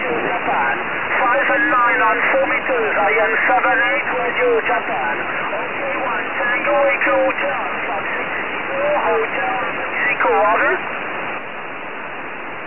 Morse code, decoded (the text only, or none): none